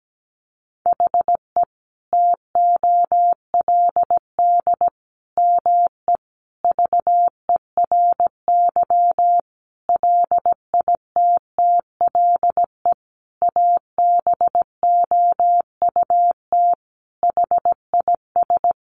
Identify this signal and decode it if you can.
Morse code, heard as a keyed tone, about 17 wpm, 700 Hz. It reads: HE TOLD ME VERY LITTLE ABOUT HIS
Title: Sentences from Top 1500 Words 17wpm